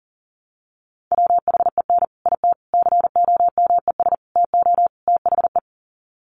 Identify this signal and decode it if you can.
Morse code, heard as a keyed tone, about 40 wpm, 700 Hz. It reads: WHEN IT COMES TO THE
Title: Sets of 5 Words 40wpm